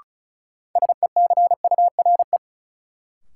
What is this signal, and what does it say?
Morse code, heard as a keyed tone, about 36 wpm, 700 Hz. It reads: SECURE
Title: Sentences from Top 1200 Words - New Word Drill Alt - 35wpm